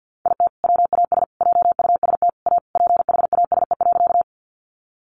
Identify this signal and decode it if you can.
Morse code, heard as a keyed tone, about 50 wpm, 700 Hz. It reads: IT WAS JUST A PHASE.
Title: Sentences from Top 1200 Words 50wpm